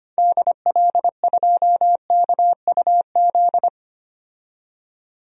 Morse code, heard as a keyed tone, about 25 wpm, 700 Hz. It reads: DL2KUZ
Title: DXCC-100 Call Signs - 6 Characters - 25wpm